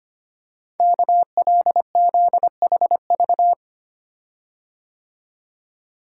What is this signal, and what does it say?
Morse code, heard as a keyed tone, about 25 wpm, 690 Hz. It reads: KLZHV
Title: Code Groups - 5 Letters - 25wpm